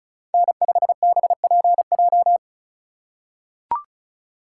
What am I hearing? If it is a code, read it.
Morse code, heard as a keyed tone, about 34 wpm, 700 Hz. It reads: N5BPJ
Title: USA Call Signs - 5 Characters - 2x Character Spacing - 35wpm